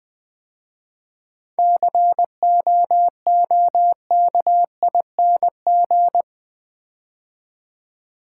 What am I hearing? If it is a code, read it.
Morse code, heard as a keyed tone, about 20 wpm, 700 Hz. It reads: COOKING
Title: 7 Letter Words - First 3 Letters Given - 20wpm